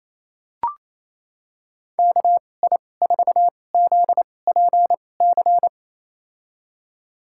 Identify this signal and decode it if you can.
Morse code, heard as a keyed tone, about 28 wpm, spaced as for 20 wpm, 700 Hz. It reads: KI4ZPC
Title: USA Call Signs - 2x Character Spacing - 28wpm